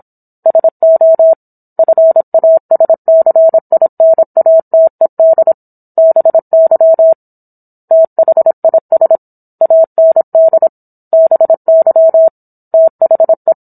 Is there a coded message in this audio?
Yes: Morse code from 0.5 to 13.5 s, heard as a keyed tone, about 27 wpm, 650 Hz. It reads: SO FASCINATED BY THIS AND BY THE